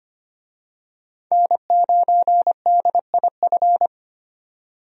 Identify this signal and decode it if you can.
Morse code, heard as a keyed tone, about 25 wpm, 700 Hz. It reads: N9DIF
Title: USA Call Signs - 5 Characters - 25wpm